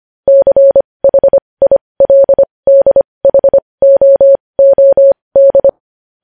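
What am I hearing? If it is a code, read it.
Morse code, heard as a keyed tone, about 25 wpm, spaced as for 20 wpm, 550 Hz. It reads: CHILDHOOD
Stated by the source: Morse code news headlines sent at 20 wpm.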